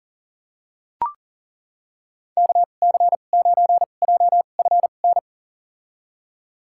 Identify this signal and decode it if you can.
Morse code, heard as a keyed tone, about 40 wpm, spaced as for 29 wpm, 690 Hz. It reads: KC9JFN
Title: USA Call Signs - 6 Characters - 2x Character Spacing - 40wpm